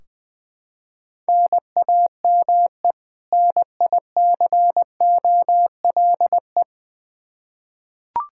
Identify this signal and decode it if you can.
Morse code, heard as a keyed tone, about 20 wpm, 700 Hz. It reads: NAME NICOLE
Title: Practice QSO - Exchange 1 - Beginning - 20wpm